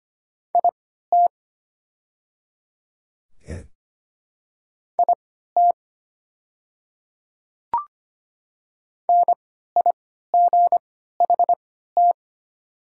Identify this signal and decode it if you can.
Morse code, heard as a keyed tone, about 25 wpm, spaced as for 14 wpm, 700 Hz. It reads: IT IT NIGHT
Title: 3x Character Spacing - Top 300 Words - 25wpm